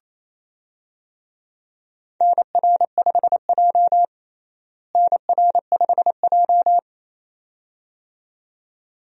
Morse code, heard as a keyed tone, about 27 wpm, 700 Hz. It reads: NR5J NR5J
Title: USA Call Signs - 4 Characters - Repeated 2x - 28wpm